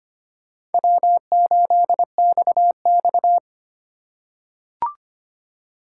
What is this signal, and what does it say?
Morse code, heard as a keyed tone, about 25 wpm, 700 Hz. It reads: W8XX
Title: USA Call Signs - 4 Characters - 4x Character Spacing - 25wpm